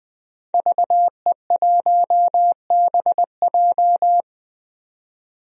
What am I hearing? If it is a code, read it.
Morse code, heard as a keyed tone, about 20 wpm, 700 Hz. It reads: VE1BJ
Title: DXCC-100 Call Signs - 5 Characters - 20wpm